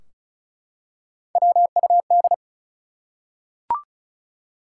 Morse code, heard as a keyed tone, about 34 wpm, 700 Hz. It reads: WUD
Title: Top 180 QSO Elements 35wpm